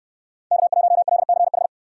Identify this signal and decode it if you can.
Morse code, heard as a keyed tone, about 69 wpm, 700 Hz. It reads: K1LCH